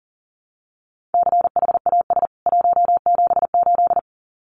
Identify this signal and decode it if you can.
Morse code, heard as a keyed tone, about 40 wpm, 700 Hz. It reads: CHAS 178
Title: CWT Exchange 40wpm